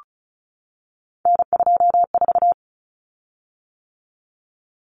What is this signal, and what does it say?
Morse code, heard as a keyed tone, about 34 wpm, 700 Hz. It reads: N24
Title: Common Cut Numbers - 3-Digit 35wpm